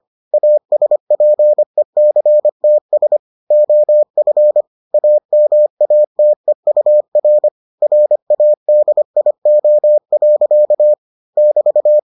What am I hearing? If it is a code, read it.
Morse code, heard as a keyed tone, about 25 wpm, 600 Hz. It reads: ASPECTS OF AMATEUR RADIO. <BT>